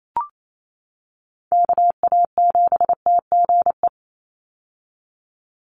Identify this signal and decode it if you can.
Morse code, heard as a keyed tone, about 28 wpm, 700 Hz. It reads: KA7TGE
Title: USA Call Signs 28wpm